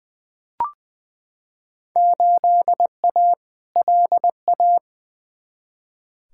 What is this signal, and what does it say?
Morse code, heard as a keyed tone, about 20 wpm, 700 Hz. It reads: 8A LA
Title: Field Day Exchanges Encore - 20wpm